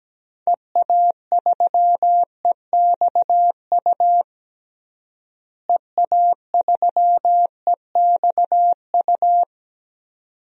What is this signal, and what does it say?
Morse code, heard as a keyed tone, about 17 wpm, 700 Hz. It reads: EA3EXU EA3EXU
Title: DXCC-100 Call Signs - 6 Characters - Repeated 2x - 17wpm